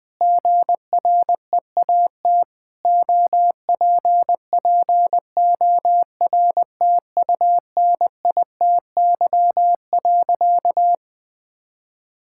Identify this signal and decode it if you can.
Morse code, heard as a keyed tone, about 20 wpm, 700 Hz. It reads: GREAT OPPORTUNITY.
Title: Sentences with 4-Word Phrases - Spoken Beginning - 20wpm